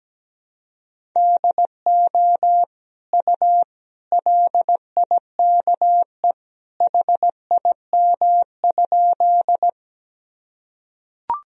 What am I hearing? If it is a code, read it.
Morse code, heard as a keyed tone, about 17 wpm, 700 Hz. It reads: DO U LIKE HIM?
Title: Sentences from Top 100 Words and Abbreviations 17wpm